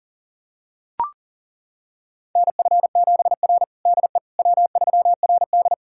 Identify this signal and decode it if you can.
Morse code, heard as a keyed tone, about 40 wpm, 700 Hz. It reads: NF7R DE W3RD
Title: USA Callsign DE Callsign - 4 Characters - 40wpm